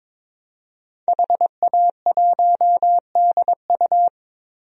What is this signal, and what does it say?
Morse code, heard as a keyed tone, about 22 wpm, 700 Hz. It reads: HA1DU
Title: DXCC-100 Call Signs - 5 Characters - Repeated 2x - 22wpm